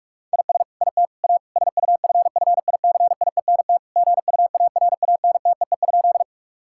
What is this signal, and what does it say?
Morse code, heard as a keyed tone, about 46 wpm, 700 Hz. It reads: IS IT A SUFFICIENT GUARANTEE?
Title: Sentences from Top 1600 Words 45wpm